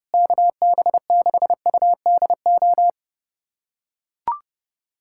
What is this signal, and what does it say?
Morse code, heard as a keyed tone, about 30 wpm, 700 Hz. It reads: KB6UDO
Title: USA Callsign DE Callsign 30wpm